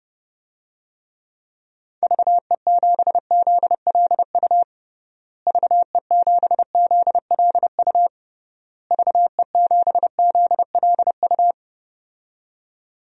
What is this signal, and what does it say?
Morse code, heard as a keyed tone, about 30 wpm, 700 Hz. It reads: VE7ZLU VE7ZLU VE7ZLU
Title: DXCC-100 Call Signs - 6 Characters - Repeated 3x - 30wpm